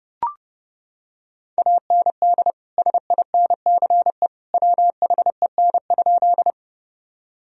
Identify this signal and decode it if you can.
Morse code, heard as a keyed tone, about 30 wpm, 700 Hz. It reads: AND SINCE WHEN?
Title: Sentences from Top 1000 Words - Review Part 1 - 30wpm